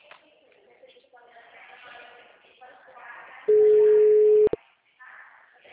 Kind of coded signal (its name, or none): telephone call progress tone